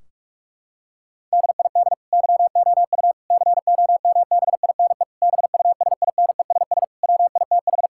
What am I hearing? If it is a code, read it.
Morse code, heard as a keyed tone, about 46 wpm, 700 Hz. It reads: DID YOU COMBINE BUSINESS WITH
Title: Sentences from Top 1600 Words 45wpm